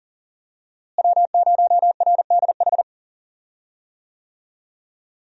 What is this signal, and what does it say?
Morse code, heard as a keyed tone, about 40 wpm, 700 Hz. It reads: W0RDH
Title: USA Call Signs - 5 Characters - Repeated 2x - 40wpm